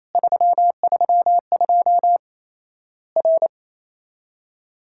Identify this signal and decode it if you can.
Morse code, heard as a keyed tone, about 28 wpm, 700 Hz. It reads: 332 R
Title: Worked All Europe DX Contest - QTC Exchange - 28wpm